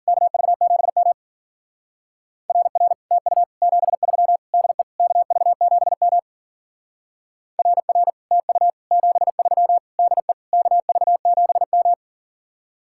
Morse code, heard as a keyed tone, about 50 wpm, 700 Hz. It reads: KV7M RR TU 73 DE KV7M RR TU 73 DE KV7M
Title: Speed-Racing Common QSO Phrases - 40wpm